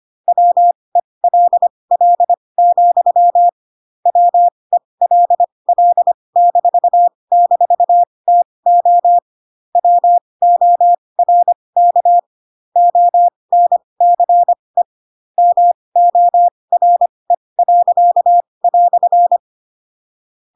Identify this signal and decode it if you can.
Morse code, heard as a keyed tone, about 25 wpm, spaced as for 20 wpm, 700 Hz. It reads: WELL, WELL--TO WORK ONCE MORE."
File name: Pinocchio / Pinocchio ch1 15 25 ik1hge